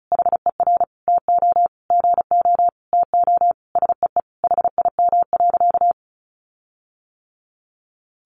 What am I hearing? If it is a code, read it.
Morse code, heard as a keyed tone, about 34 wpm, 700 Hz. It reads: HER TO GO TO SEE HIM.
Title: Sentences from Top 1500 Words 35wpm